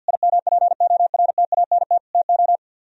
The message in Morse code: IMPORTANT TO

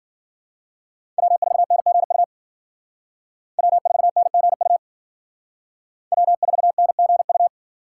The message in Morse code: W4NGU W4NGU W4NGU

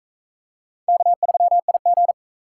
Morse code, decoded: K3IG

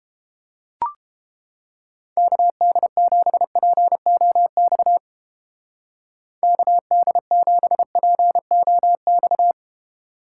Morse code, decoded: KD7POX KD7POX

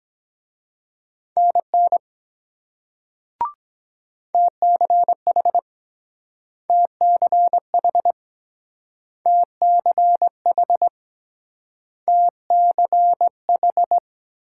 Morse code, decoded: NN TCH TCH TCH TCH

